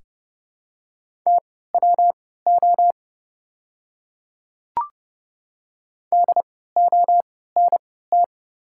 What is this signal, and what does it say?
Morse code, heard as a keyed tone, about 30 wpm, spaced as for 17 wpm, 690 Hz. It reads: TWO DONT
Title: 3x Character Spacing - Top 300 Words - 30wpm